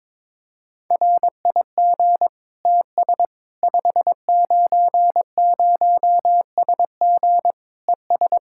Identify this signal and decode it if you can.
Morse code, heard as a keyed tone, about 22 wpm, 700 Hz. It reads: RIG TS 590SG ES
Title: Practice QSO - Exchange 2 - Info - 22wpm